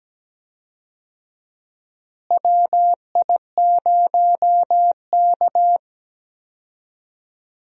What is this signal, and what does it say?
Morse code, heard as a keyed tone, about 17 wpm, 700 Hz. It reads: WI0K
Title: USA Call Signs - 4 Characters - Repeated 2x - 17wpm